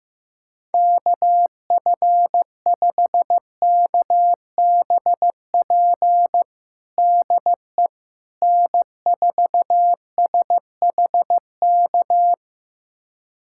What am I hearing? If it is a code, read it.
Morse code, heard as a keyed tone, about 15 wpm, 700 Hz. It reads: KF5KBP DE N4SHK
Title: USA Callsign DE Callsign 15wpm